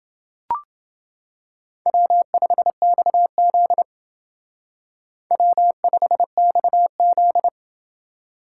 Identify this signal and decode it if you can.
Morse code, heard as a keyed tone, about 28 wpm, 700 Hz. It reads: W5XZ W5XZ